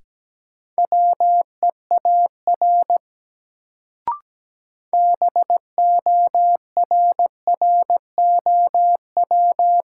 Morse code, heard as a keyed tone, about 17 wpm, 700 Hz. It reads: WEAR BORROW
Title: Sentences from Top 1500 Words - New Word Drill Alt - 17wpm